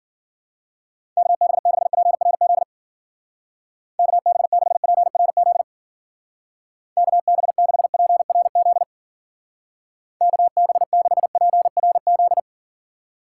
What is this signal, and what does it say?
Morse code, heard as a keyed tone, about 52 wpm, 700 Hz. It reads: KB6PRZ KB6PRZ KB6PRZ KB6PRZ